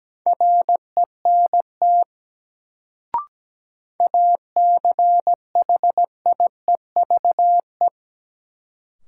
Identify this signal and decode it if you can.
Morse code, heard as a keyed tone, about 17 wpm, 700 Hz. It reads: RENT ACHIEVE
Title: Sentences from Top 1000 Words - Review Part 5 - New Word Drill - 17wpm